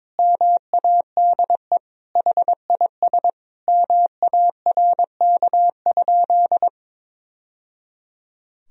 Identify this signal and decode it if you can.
Morse code, heard as a keyed tone, about 21 wpm, 700 Hz. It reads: MADE HIS MARK?
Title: Sentences from Top 1000 Words - Review Part 1 - 22wpm